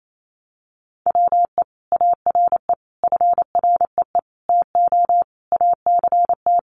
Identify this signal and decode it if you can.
Morse code, heard as a keyed tone, about 28 wpm, 700 Hz. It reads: WE ARE FREE TO ACT
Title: Sentences from Top 500 Words 28wpm